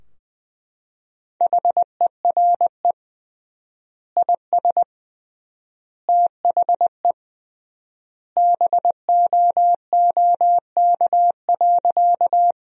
Morse code, heard as a keyed tone, about 21 wpm, 700 Hz. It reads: HERE IS THE BOOK.
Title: 3x Word-Spacing - Sentences from Top 300 Words - 20wpm